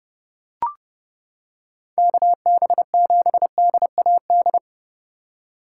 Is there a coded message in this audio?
Yes: Morse code from 2.0 to 4.6 s, heard as a keyed tone, about 29 wpm, 700 Hz. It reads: KB7DAD